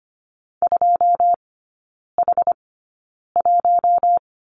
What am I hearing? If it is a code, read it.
Morse code, heard as a keyed tone, about 25 wpm, spaced as for 9 wpm, 700 Hz. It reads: 2H1
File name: Lesson 14e - Number 2 - Three Characters 25wpm